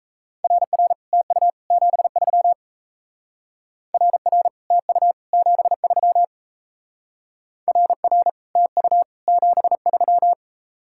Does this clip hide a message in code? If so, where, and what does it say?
Yes: Morse code from 0.4 to 10.3 s, heard as a keyed tone, about 36 wpm, 700 Hz. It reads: RR TU 73 RR TU 73 RR TU 73